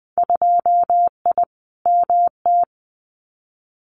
Morse code, heard as a keyed tone, about 20 wpm, 700 Hz. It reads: 2I MT
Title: Winter Field Day Exchanges Encore 20wpm